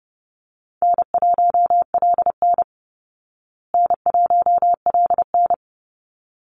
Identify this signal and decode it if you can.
Morse code, heard as a keyed tone, about 30 wpm, 700 Hz. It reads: N1LN N1LN
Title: Call Signs Repeated 3x - 30wpm